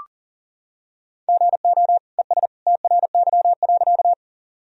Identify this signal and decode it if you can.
Morse code, heard as a keyed tone, about 39 wpm, 700 Hz. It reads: GO ES TRY.
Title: Sentences from Top 200 Words and Abbreviations 40wpm